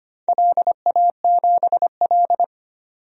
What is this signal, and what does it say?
Morse code, heard as a keyed tone, about 25 wpm, 700 Hz. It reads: LA7L
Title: DXCC-100 Call Signs - 4 Characters - 25wpm